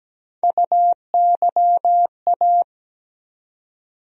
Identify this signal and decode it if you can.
Morse code, heard as a keyed tone, about 17 wpm, 700 Hz. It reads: UYA
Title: USA Call Sign Postfixes - 3 Letters 17wpm